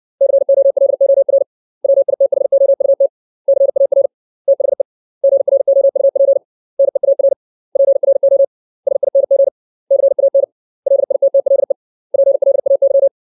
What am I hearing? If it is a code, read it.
Morse code, heard as a keyed tone, about 64 wpm, 550 Hz. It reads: COLOR WITHOUT CAR THE GROUP NEAR PRO SING CAN LITTLE PLAY
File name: 632 words / 632 Words 60WPM